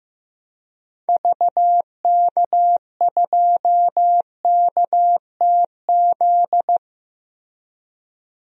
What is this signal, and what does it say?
Morse code, heard as a keyed tone, about 15 wpm, 700 Hz. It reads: VK2KTZ
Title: DXCC-100 Call Signs - 6 Characters - Repeated 2x - 15wpm